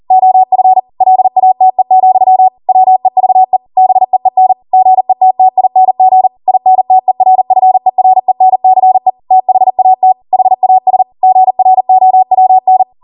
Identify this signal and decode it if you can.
Morse code, heard as a keyed tone, about 40 wpm, 750 Hz. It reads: OF LATE, WEVE BEEN GETTING INTERFERENCE THAT HAS GROWN
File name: Morse / 40WPM